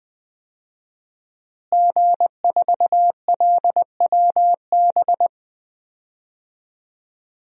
Morse code, heard as a keyed tone, about 20 wpm, 700 Hz. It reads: G4LWB
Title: DXCC-100 Call Signs - 5 Characters - Repeated 3x - 20wpm